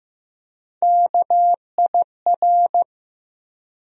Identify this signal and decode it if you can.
Morse code, heard as a keyed tone, about 15 wpm, 700 Hz. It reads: KIR
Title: USA Call Sign Postfixes - 3 Letters 15wpm